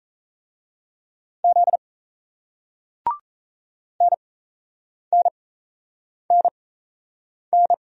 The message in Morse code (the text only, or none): Z NNNN